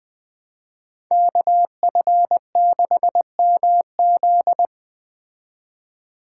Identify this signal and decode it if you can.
Morse code, heard as a keyed tone, about 20 wpm, 700 Hz. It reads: KF6MZ